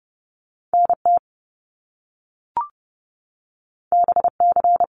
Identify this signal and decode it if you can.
Morse code, heard as a keyed tone, about 30 wpm, 700 Hz. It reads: NT BC
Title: Canadian Provinces and Territories 30wpm